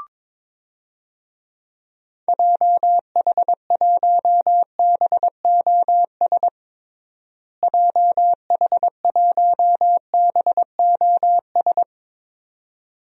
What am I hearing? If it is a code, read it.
Morse code, heard as a keyed tone, about 22 wpm, 700 Hz. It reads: JH1BOS JH1BOS
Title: DXCC-100 Call Signs - 6 Characters - Repeated 2x - 22wpm